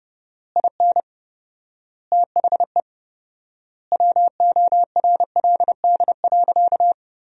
Morse code, heard as a keyed tone, about 32 wpm, 700 Hz. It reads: IN THE WORLD.
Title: 4x Word-Spacing - Sentences from Top 200 Words - 30wpm